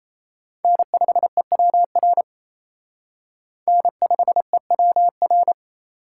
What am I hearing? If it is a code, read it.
Morse code, heard as a keyed tone, about 36 wpm, 700 Hz. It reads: N5EWR N5EWR